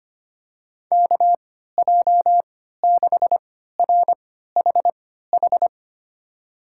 Morse code, heard as a keyed tone, about 25 wpm, spaced as for 14 wpm, 700 Hz. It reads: KJ6RHH
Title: USA Call Signs - 6 Characters - 3x Character Spacing - 25wpm